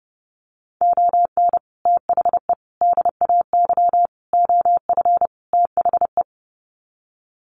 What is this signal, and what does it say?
Morse code, heard as a keyed tone, about 30 wpm, 700 Hz. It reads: ON THE DAY OF THE
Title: Sets of 5 Words Encore 30wpm